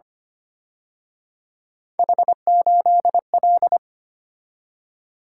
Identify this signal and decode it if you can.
Morse code, heard as a keyed tone, about 29 wpm, 700 Hz. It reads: H8L
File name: Speed-Racing - Three Letter-Number Combinations - 25wpm